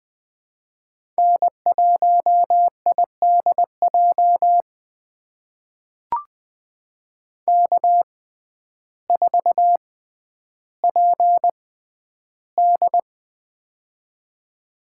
Morse code, heard as a keyed tone, about 20 wpm, spaced as for 11 wpm, 700 Hz. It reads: N1IDJ K4PD
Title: USA Call Signs - 6x Character Spacing - 20wpm